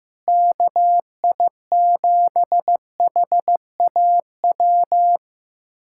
Morse code, heard as a keyed tone, about 15 wpm, 700 Hz. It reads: KI7HAW
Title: USA Callsign DE Callsign - 6 Characters - 15wpm